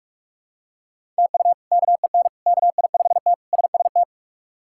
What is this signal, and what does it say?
Morse code, heard as a keyed tone, about 46 wpm, 700 Hz. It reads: TU KEN KI5T SST
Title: K1USN SST QSO Practice - 45wpm